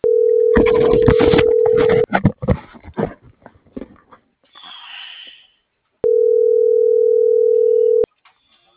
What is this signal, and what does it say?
Ringback tone (North America), 440 + 480 Hz.